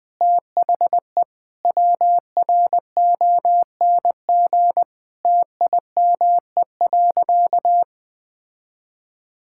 Morse code, heard as a keyed tone, about 20 wpm, 700 Hz. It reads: THE WRONG TIME.
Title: Sentences with 9-Word Phrases - 20wpm